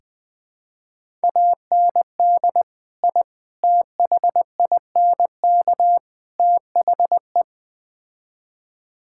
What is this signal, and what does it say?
Morse code, heard as a keyed tone, about 20 wpm, 700 Hz. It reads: AND I THINK THE
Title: Build A Phrase - 4-Word Phrases - 20wpm